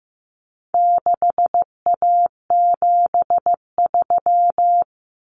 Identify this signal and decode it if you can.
Morse code, heard as a keyed tone, about 15 wpm, 700 Hz. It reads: 6A73
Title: Common Cut Numbers - 4-Digit 15wpm